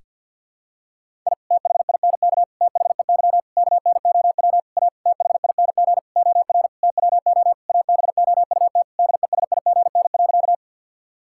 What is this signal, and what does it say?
Morse code, heard as a keyed tone, about 51 wpm, 700 Hz. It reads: I THINK THEY KNOW A THING OR TWO ABOUT DESIGN.
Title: Sentences from Top 1000 Words - Review Part 5 - 50wpm